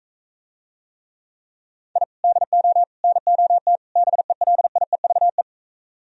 Morse code, heard as a keyed tone, about 42 wpm, 700 Hz. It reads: I DO NOT BELIEVE